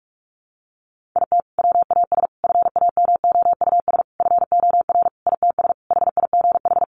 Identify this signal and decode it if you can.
Morse code, heard as a keyed tone, about 44 wpm, 700 Hz. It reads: IT WAS FAMOUS FOR ITS HIGH
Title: Sentences from Top 1500 Words 45wpm